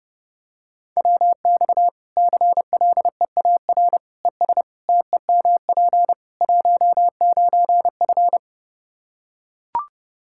Morse code, heard as a keyed tone, about 30 wpm, 700 Hz. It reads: WX CLEAR ES TEMP 19F
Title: Practice QSO - Exchange 2 - Info - 30wpm